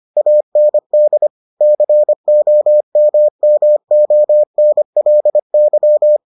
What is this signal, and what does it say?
Morse code, heard as a keyed tone, about 25 wpm, 600 Hz. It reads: AND COMMONLY